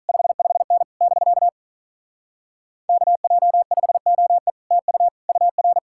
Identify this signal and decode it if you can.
Morse code, heard as a keyed tone, about 46 wpm, 700 Hz. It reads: 55N <BK> KJ5OE TU UR